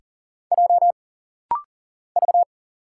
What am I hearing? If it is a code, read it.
Morse code, heard as a keyed tone, about 40 wpm, 700 Hz. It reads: J V
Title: Single Letter-Number - Rapid-Fire 40wpm